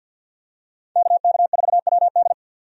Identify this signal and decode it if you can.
Morse code, heard as a keyed tone, about 52 wpm, 700 Hz. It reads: KK4WD
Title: USA Call Signs - 5 Characters - 50wpm